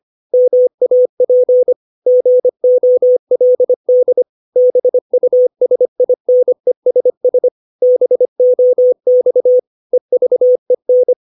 Morse code, heard as a keyed tone, about 25 wpm, 500 Hz. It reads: MAP GOLD BUSINESS BOX EVEN
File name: 632 words / 632 Words 25WPM